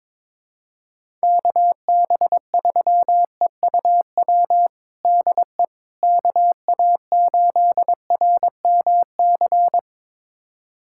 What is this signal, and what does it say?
Morse code, heard as a keyed tone, about 22 wpm, 700 Hz. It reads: KB3EUW DE KA8RMC
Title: USA Callsign DE Callsign - 6 Characters - 22wpm